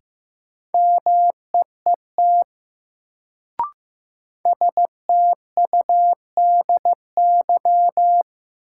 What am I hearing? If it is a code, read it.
Morse code, heard as a keyed tone, about 15 wpm, 700 Hz. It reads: MEET STUDY
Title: Sets of 5 Words - New Word Drill - 15wpm